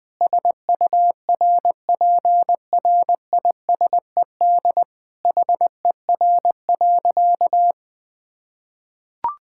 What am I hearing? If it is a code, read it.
Morse code, heard as a keyed tone, about 21 wpm, 700 Hz. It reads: SURPRISED HER.
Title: Sentences with 4-Word Phrases - 20wpm